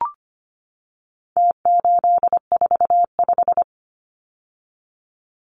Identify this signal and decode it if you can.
Morse code, heard as a keyed tone, about 25 wpm, 700 Hz. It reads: T845
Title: Numbers - Contest Style - 4-Digit 25wpm